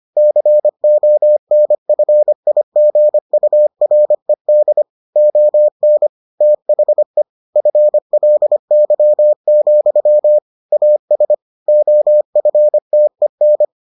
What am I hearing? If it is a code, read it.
Morse code, heard as a keyed tone, about 25 wpm, 600 Hz. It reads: CONFIGURED ON THE FLY, AS OFTEN